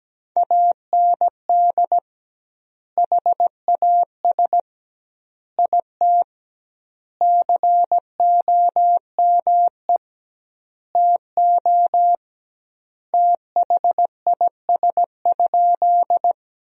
Morse code, heard as a keyed tone, about 17 wpm, 700 Hz. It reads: AND HAS IT COME TO THIS?
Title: 2x Word-Spacing - Sentences from Top 100 Words - 17wpm